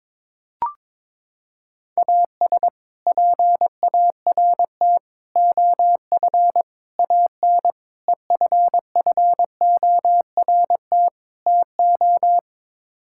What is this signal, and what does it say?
Morse code, heard as a keyed tone, about 22 wpm, 700 Hz. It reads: AS PART OF AN EFFORT TO
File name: practice / 22wpm / Sets of 6 Words - 22wpm